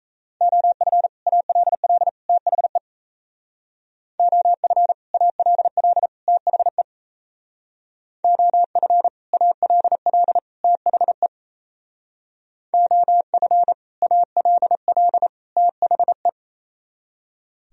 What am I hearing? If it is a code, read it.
Morse code, heard as a keyed tone, about 33 wpm, 700 Hz. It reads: OF ALL THE OF ALL THE OF ALL THE OF ALL THE
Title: Speed-Racing - Sets of 3 Words 28wpm